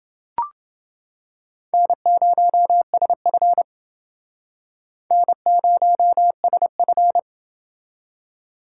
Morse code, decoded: N0SF N0SF

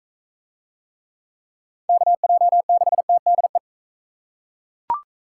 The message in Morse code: KJ6TDE

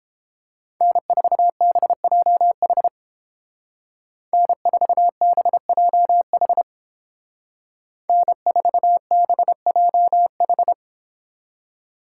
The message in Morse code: N4BJH N4BJH N4BJH

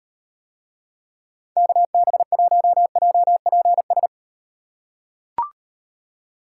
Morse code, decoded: KB1JPS

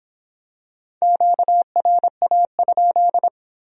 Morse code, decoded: QRA?

